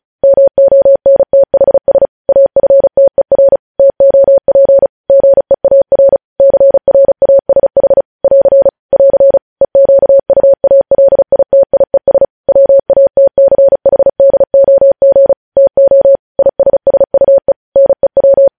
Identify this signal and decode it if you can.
Morse code, heard as a keyed tone, about 36 wpm, 550 Hz. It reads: MONTHS AFTER TOP GEAR CRASH <AR> <AR> EQUALITIES WATCHDOG TO ISSUE NEW
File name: podcast / News headlines-1x-35x35